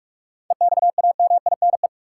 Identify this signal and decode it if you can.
Morse code, heard as a keyed tone, about 43 wpm, 700 Hz. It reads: EXAMINE
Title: Sentences from Top 1600 Words - New Word Drill - 45wpm